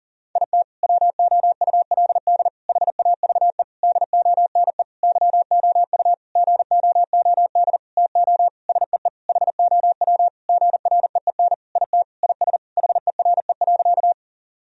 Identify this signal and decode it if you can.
Morse code, heard as a keyed tone, about 40 wpm, 700 Hz. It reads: IT WOULD HAVE DONE YOU GOOD TO SEE HOW GREEN IT IS HERE.
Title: Sentences from Top 1000 Words - Review Part 2 - 40wpm